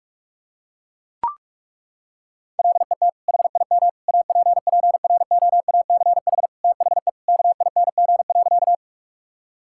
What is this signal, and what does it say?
Morse code, heard as a keyed tone, about 41 wpm, 700 Hz. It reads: LET HIM APPROACH THE KING.
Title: Sentences from Top 1000 Words - Review Part 6 - 45wpm